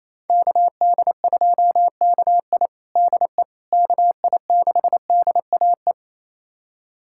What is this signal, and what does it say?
Morse code, heard as a keyed tone, about 27 wpm, 700 Hz. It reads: KD2KI DE KI6DAE
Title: USA Callsign DE Callsign 28wpm